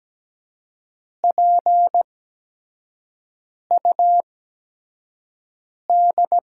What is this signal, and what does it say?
Morse code, heard as a keyed tone, about 17 wpm, spaced as for 5 wpm, 700 Hz. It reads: PUD
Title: USA Call Signs - 5 Characters - 8x Character Spacing - 17wpm